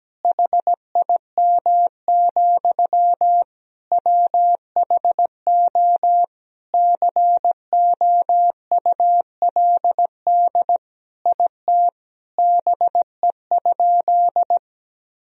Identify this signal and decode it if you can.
Morse code, heard as a keyed tone, about 17 wpm, 700 Hz. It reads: HIM, WHO COULD IT BE?
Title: Sentences from Top 1000 Words - Review Part 2 - 17wpm